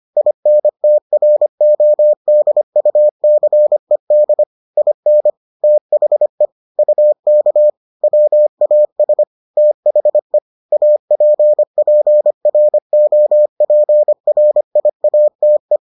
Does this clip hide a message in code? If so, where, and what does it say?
Yes: Morse code from 0.2 to 15.8 s, heard as a keyed tone, about 25 wpm, 600 Hz. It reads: INTRODUCED IN THE UK WAS THE APPROPRIATE